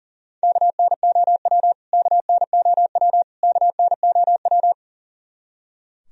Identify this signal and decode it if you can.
Morse code, heard as a keyed tone, about 40 wpm, 700 Hz. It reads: KNOW KNOW KNOW
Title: Top 100 Words Repeated 40wpm